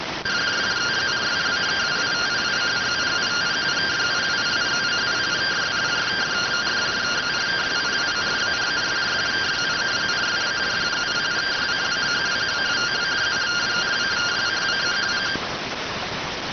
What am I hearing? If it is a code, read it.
Two-tone teleprinter signal, about 170 Hz shift.